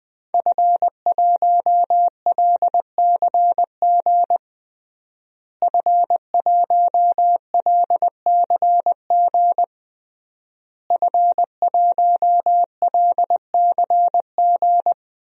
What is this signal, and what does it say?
Morse code, heard as a keyed tone, about 19 wpm, 700 Hz. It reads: F1LCG F1LCG F1LCG
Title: DXCC-100 Call Signs - 5 Characters - Repeated 3x - 20wpm